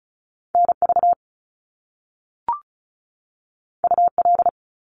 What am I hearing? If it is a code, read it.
Morse code, heard as a keyed tone, about 36 wpm, 700 Hz. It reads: NV UL
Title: Top Two Letter Combinations in Words 35wpm